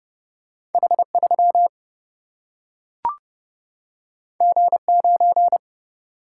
Morse code, heard as a keyed tone, about 30 wpm, 700 Hz. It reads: H 3 G 9
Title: Single-Letter Single-Number 30wpm